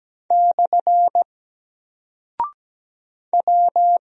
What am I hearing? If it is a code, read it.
Morse code, heard as a keyed tone, about 17 wpm, 700 Hz. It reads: / W
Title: Single Character (Letter:Number:Punctuation:Prosign) Alt 17wpm